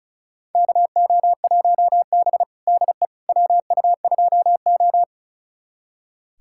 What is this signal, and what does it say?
Morse code, heard as a keyed tone, about 36 wpm, 700 Hz. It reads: KO1B DE WU2O
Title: USA Callsign DE Callsign - 4 Characters - 35wpm